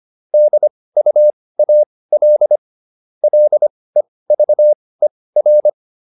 Morse code, heard as a keyed tone, about 25 wpm, spaced as for 18 wpm, 600 Hz. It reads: DUAL LEVER